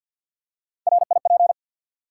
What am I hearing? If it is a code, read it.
Morse code, heard as a keyed tone, about 52 wpm, 700 Hz. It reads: RIP
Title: Three Letter Words - 50wpm